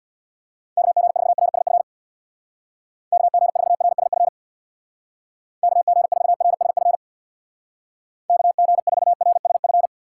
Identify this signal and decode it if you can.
Morse code, heard as a keyed tone, about 65 wpm, 700 Hz. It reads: KG4RSF KG4RSF KG4RSF KG4RSF